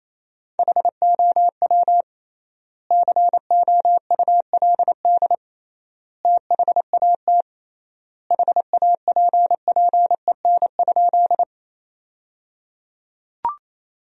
Morse code, heard as a keyed tone, about 28 wpm, 700 Hz. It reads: HOW COULD THAT HAPPEN?
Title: 3x Word-Spacing - Sentences from Top 400 Words - 28wpm